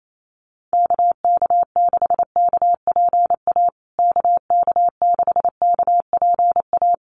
Morse code, heard as a keyed tone, about 28 wpm, 700 Hz. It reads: KK6KPA KK6KPA
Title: USA Call Signs Repeated 3x 28wpm